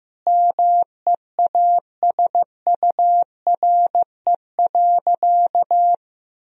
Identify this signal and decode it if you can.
Morse code, heard as a keyed tone, about 15 wpm, 700 Hz. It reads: MEASURE.
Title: Sentences from Top 1000 Words - Review Part 3 - 15wpm